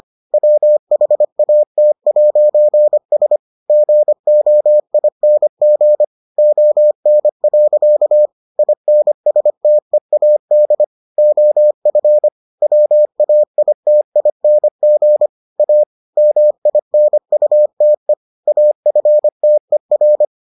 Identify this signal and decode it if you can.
Morse code, heard as a keyed tone, about 25 wpm, 600 Hz. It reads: WHAT'S GOING ON. INSTEAD OF WAITING A MINUTE AFTER